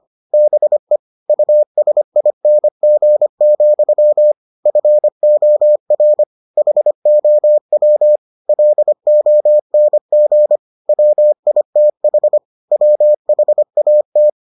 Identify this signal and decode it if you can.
Morse code, heard as a keyed tone, about 25 wpm, 600 Hz. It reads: BE USING, FOR HOW LONG WITH WHAT